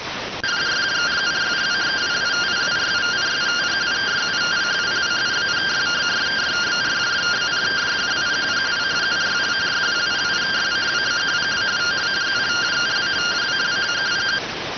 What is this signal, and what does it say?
Two-tone teleprinter signal, about 170 Hz shift.